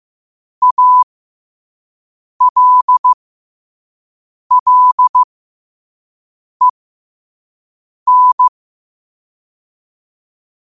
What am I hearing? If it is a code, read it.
Morse code, heard as a keyed tone, about 15 wpm, spaced as for 5 wpm, 1000 Hz. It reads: ALLEN
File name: 5wpm